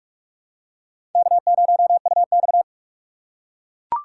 Morse code, heard as a keyed tone, about 43 wpm, 700 Hz. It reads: K0UX
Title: USA Call Signs - 4 Characters - 3x Character Spacing - 45wpm